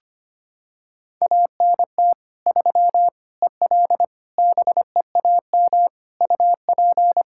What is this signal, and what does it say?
Morse code, heard as a keyed tone, about 25 wpm, 700 Hz. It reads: ANT 3 EL BEAM UP
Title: Practice QSO - Exchange 2 - Info - 25wpm